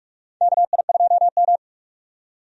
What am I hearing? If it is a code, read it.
Morse code, heard as a keyed tone, about 43 wpm, 700 Hz. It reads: KI2M
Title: USA Call Signs - 4 Characters - 45wpm